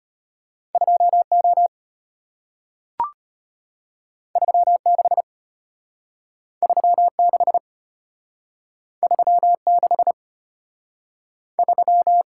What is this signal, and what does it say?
Morse code, heard as a keyed tone, about 34 wpm, 700 Hz. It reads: 2O 36 36 36 3